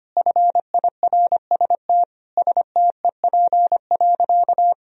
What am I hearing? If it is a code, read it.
Morse code, heard as a keyed tone, about 25 wpm, 700 Hz. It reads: FIRST STEP.
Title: Sentences from Top 1000 Words - Review Part 2 - 25wpm